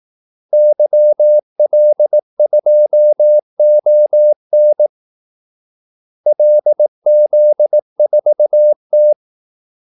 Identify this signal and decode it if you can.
Morse code, heard as a keyed tone, about 18 wpm, 600 Hz. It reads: YL2ON LZ4T
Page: Kaldesignaler 18 wpm